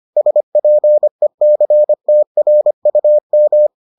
Morse code, heard as a keyed tone, about 25 wpm, 600 Hz. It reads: SPECTRUM